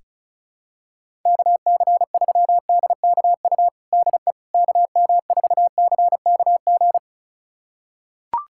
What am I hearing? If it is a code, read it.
Morse code, heard as a keyed tone, about 34 wpm, 700 Hz. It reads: KC3DKU DE KM4CKG
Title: USA Callsign DE Callsign 35wpm